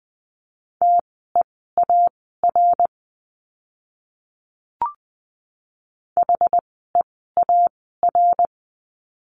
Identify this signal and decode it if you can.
Morse code, heard as a keyed tone, about 20 wpm, spaced as for 14 wpm, 700 Hz. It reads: TEAR HEAR
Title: 068 Lesson 12f - Letter d - Words 20wpm